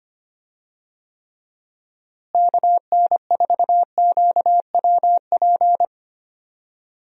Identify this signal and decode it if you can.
Morse code, heard as a keyed tone, about 25 wpm, 700 Hz. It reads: KN4QWP
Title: USA Call Signs - 6 Characters - Repeated 3x - 25wpm